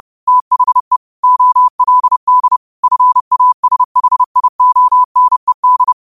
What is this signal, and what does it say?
Morse code, heard as a keyed tone, about 30 wpm, 1000 Hz. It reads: THE OLD FASHIONED
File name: qotdfiles / 30wpm